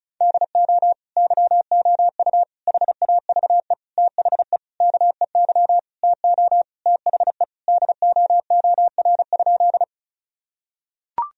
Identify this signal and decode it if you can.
Morse code, heard as a keyed tone, about 34 wpm, 700 Hz. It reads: DO YOU HAVE THE KEY TO THE DOOR?
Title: Sentences from Top 1000 Words - Review Part 4 - 35wpm